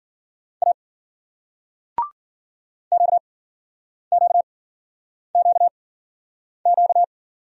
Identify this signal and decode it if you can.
Morse code, heard as a keyed tone, about 54 wpm, spaced as for 9 wpm, 700 Hz. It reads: A QQQQ